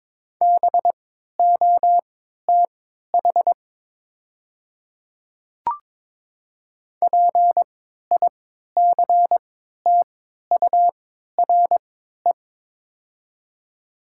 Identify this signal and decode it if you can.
Morse code, heard as a keyed tone, about 22 wpm, spaced as for 12 wpm, 690 Hz. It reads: BOTH PICTURE
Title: 3x Character Spacing - Top 300 Words - 22wpm